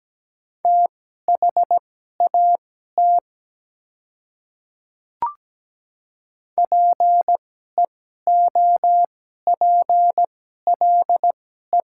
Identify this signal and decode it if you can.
Morse code, heard as a keyed tone, about 17 wpm, spaced as for 12 wpm, 700 Hz. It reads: THAT PEOPLE
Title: 2x Character Spacing - Top 200 Words - 17wpm